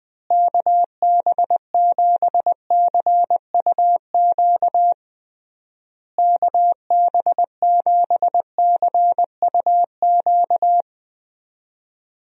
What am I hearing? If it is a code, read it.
Morse code, heard as a keyed tone, about 20 wpm, 700 Hz. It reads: KB7CUQ KB7CUQ
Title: USA Call Signs - 6 Characters - Repeated 3x - 20wpm